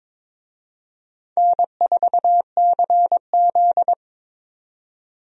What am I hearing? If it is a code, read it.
Morse code, heard as a keyed tone, about 22 wpm, 700 Hz. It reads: N4CZ
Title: USA Call Signs - 4 Characters - 22wpm